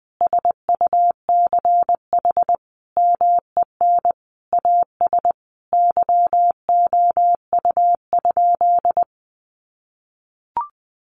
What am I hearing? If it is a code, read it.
Morse code, heard as a keyed tone, about 20 wpm, 700 Hz. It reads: SUCH MEN AS YOU?
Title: Sentences from Top 1000 Words 20wpm